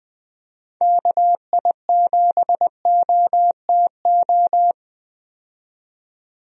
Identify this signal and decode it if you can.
Morse code, heard as a keyed tone, about 21 wpm, 700 Hz. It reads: KI7OTO
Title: USA Call Signs - 6 Characters - 20wpm